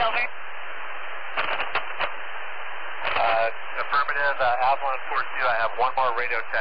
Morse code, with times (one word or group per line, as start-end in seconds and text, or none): none